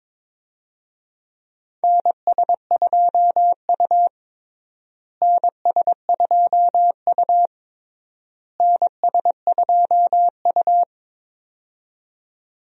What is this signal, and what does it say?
Morse code, heard as a keyed tone, about 22 wpm, 700 Hz. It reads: NS2U NS2U NS2U
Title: USA Call Signs - 4 Characters - Repeated 3x - 22wpm